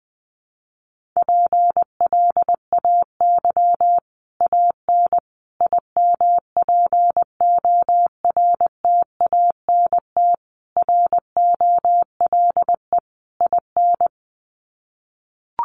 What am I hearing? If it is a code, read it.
Morse code, heard as a keyed tone, about 20 wpm, 700 Hz. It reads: PLAY AN IMPORTANT ROLE IN
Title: Sets of 5 Words Encore 20wpm